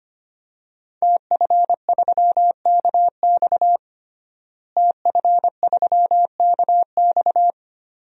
Morse code, heard as a keyed tone, about 25 wpm, 700 Hz. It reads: TF3KX TF3KX
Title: DXCC-100 Call Signs - 5 Characters - Repeated 2x - 25wpm